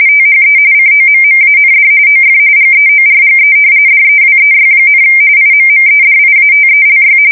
RTTY, about 45.45 baud. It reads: THE COMMAND OF GEORGE DEWEY, IMMEDIATELY
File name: fsk170 / usnavy3